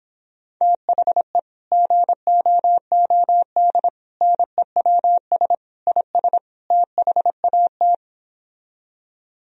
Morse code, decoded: THE GOOD NEWS IS THAT

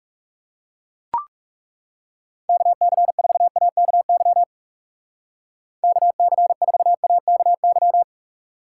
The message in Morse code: KC4AKY KC4AKY

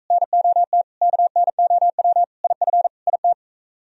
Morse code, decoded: NOT KNOW IF IT

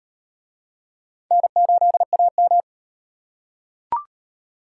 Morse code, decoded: N8AM

